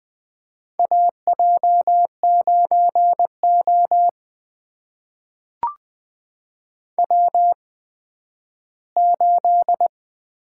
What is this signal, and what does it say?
Morse code, heard as a keyed tone, about 20 wpm, spaced as for 12 wpm, 700 Hz. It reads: AJ9O W8
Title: USA Call Signs - 4 Characters - 8x Character Spacing - 20wpm